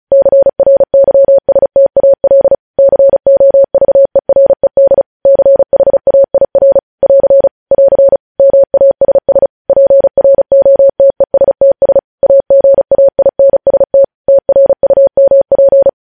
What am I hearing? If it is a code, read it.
Morse code, heard as a keyed tone, about 36 wpm, 550 Hz. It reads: CRYSTAL COVERED CHAIR <AR> <AR> MASS PROTESTS AGAINST TRUMP
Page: news headlines in morse code at 35 wpm